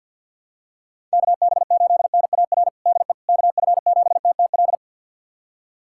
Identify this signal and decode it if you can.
Morse code, heard as a keyed tone, about 52 wpm, 700 Hz. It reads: KB8NAR DE KF7TTL
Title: USA Callsign DE Callsign - 6 Characters - 50wpm